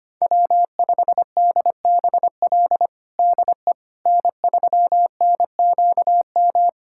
Morse code, heard as a keyed tone, about 25 wpm, 700 Hz. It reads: W5DBL DE N3NQM
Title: USA Callsign DE Callsign - 5 Characters - 25wpm